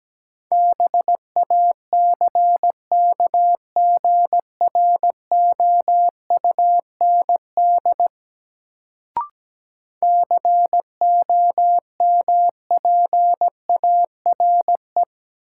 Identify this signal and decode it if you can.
Morse code, heard as a keyed tone, about 17 wpm, 700 Hz. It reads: BACKGROUND COMPARE
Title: Sentences from Top 1300 Words - New Word Drill - 17wpm